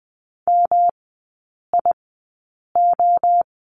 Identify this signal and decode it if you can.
Morse code, heard as a keyed tone, about 20 wpm, spaced as for 8 wpm, 700 Hz. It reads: MIO
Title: 121 Lesson 20e - Number 3 - Three Characters 20wpm